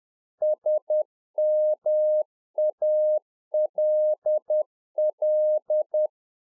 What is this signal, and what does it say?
Morse code, heard as a keyed tone, about 10 wpm, 610 Hz. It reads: SMALL